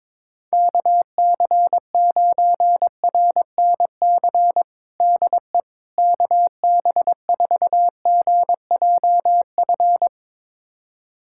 Morse code, heard as a keyed tone, about 22 wpm, 700 Hz. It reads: KC9RNC DE KB4GJF
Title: USA Callsign DE Callsign 22wpm